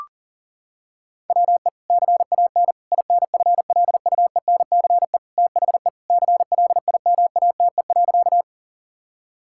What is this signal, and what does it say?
Morse code, heard as a keyed tone, about 40 wpm, 700 Hz. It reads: WE CAN INFLUENCE THE CLIMATE.
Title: Sentences from Top 1600 Words 40wpm